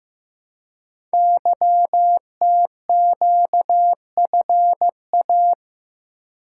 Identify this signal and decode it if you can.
Morse code, heard as a keyed tone, about 15 wpm, 700 Hz. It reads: YTQFA
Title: Code Groups - 5 Letters - 15wpm